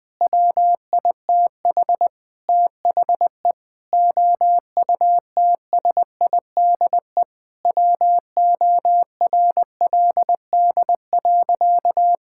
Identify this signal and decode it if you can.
Morse code, heard as a keyed tone, about 19 wpm, 700 Hz. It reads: WITH THE OUTSIDE WORLD.
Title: Sentences from Top 1000 Words - Review Part 6 - 20wpm